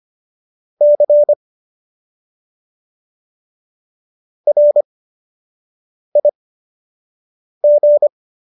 Morse code, heard as a keyed tone, about 25 wpm, spaced as for 6 wpm, 590 Hz. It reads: C RIG